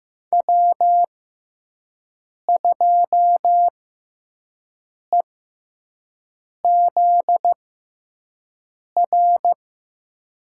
Morse code, heard as a keyed tone, about 15 wpm, spaced as for 5 wpm, 700 Hz. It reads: W2EZR
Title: USA Call Signs - 5 Characters - 6x Character Spacing - 15wpm